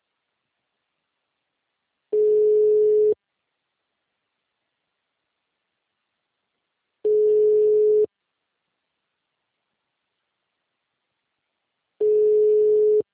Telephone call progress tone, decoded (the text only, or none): ringback tone